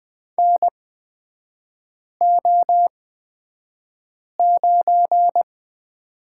Morse code, heard as a keyed tone, about 20 wpm, spaced as for 5 wpm, 690 Hz. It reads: NO9
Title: 229 Lesson 37e - Letter x - Three Characters 20wpm